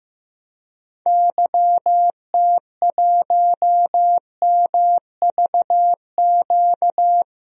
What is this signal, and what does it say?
Morse code, heard as a keyed tone, about 16 wpm, 700 Hz. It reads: YT1MVQ
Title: DXCC-100 Call Signs - 6 Characters - Repeated 3x - 15wpm